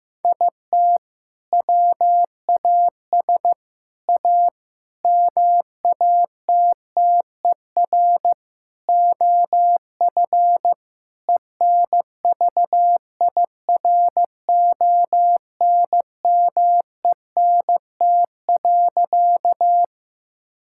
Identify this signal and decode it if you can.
Morse code, heard as a keyed tone, about 15 wpm, 700 Hz. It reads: IT WAS A MATTER OF ENVIRONMENT.
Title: Sentences from Top 1000 Words - Review Part 5 - 15wpm